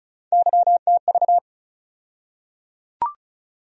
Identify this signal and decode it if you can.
Morse code, heard as a keyed tone, about 34 wpm, 700 Hz. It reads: YTV
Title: USA Call Sign Postfixes - 3 Letters 35wpm